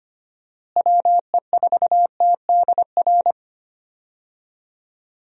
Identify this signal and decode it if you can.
Morse code, heard as a keyed tone, about 25 wpm, 700 Hz. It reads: WE4TDR
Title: USA Call Signs - 6 Characters - Repeated 3x - 25wpm